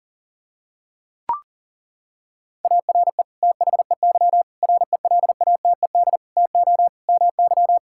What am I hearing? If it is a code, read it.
Morse code, heard as a keyed tone, about 37 wpm, 700 Hz. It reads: ARE THEY RELATED TO MY
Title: Sentences from Top 1000 Words - Review Part 5 - 40wpm